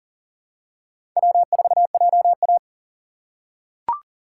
Morse code, heard as a keyed tone, about 40 wpm, 700 Hz. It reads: W4JA
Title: USA Call Signs - 4x Character Spacing - 40wpm